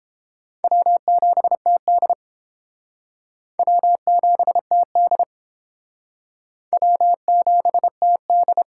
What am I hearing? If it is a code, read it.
Morse code, heard as a keyed tone, about 30 wpm, 700 Hz. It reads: W7TD W7TD W7TD